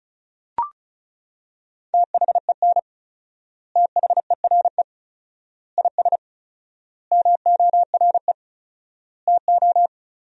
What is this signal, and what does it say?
Morse code, heard as a keyed tone, about 36 wpm, 700 Hz. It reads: THEN THERE IS MORE TO
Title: 4x Word-Spacing - Sentences from Top 100 Words - 35wpm